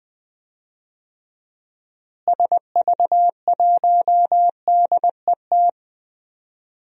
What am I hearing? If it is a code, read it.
Morse code, heard as a keyed tone, about 21 wpm, 700 Hz. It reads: SV1DET
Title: DXCC-100 Call Signs - 6 Characters - Repeated 3x - 20wpm